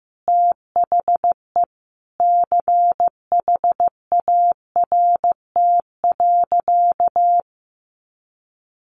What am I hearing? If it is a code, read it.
Morse code, heard as a keyed tone, about 15 wpm, 700 Hz. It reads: THE CHART.
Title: Sentences from Top 1300 Words 15wpm